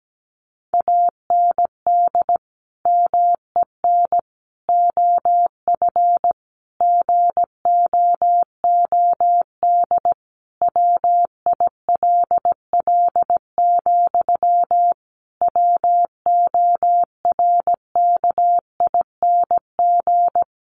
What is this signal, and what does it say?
Morse code, heard as a keyed tone, about 17 wpm, 700 Hz. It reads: AND MEN OF GOOD WILL, WORKING
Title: Sentences from Top 1400 Words 17wpm